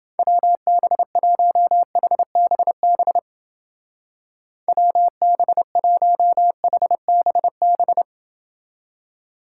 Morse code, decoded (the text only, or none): WB1HBB WB1HBB